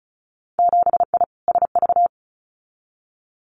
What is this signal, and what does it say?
Morse code, heard as a keyed tone, about 36 wpm, 700 Hz. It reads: 7I SV
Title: Winter Field Day Exchanges Encore 35wpm